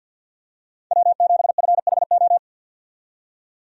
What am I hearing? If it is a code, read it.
Morse code, heard as a keyed tone, about 50 wpm, 700 Hz. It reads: W7FHO
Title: USA Call Signs - 5 Characters - 50wpm